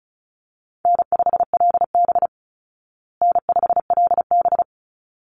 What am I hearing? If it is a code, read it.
Morse code, heard as a keyed tone, about 34 wpm, 700 Hz. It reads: N5LB N5LB
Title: Easy Call Signs Repeated 2x - 35wpm